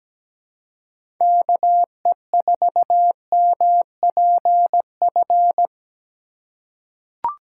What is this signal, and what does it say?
Morse code, heard as a keyed tone, about 17 wpm, 700 Hz. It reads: KE4MPF
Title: USA Call Signs - 6 Characters - 4x Character Spacing - 17wpm